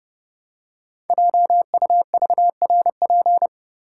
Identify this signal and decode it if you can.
Morse code, heard as a keyed tone, about 30 wpm, 700 Hz. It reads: JUVRP
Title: Code Groups - 5 Letters - 30wpm